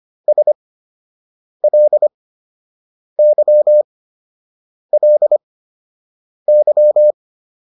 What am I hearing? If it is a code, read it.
Morse code, heard as a keyed tone, about 25 wpm, spaced as for 7 wpm, 600 Hz. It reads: SLYLY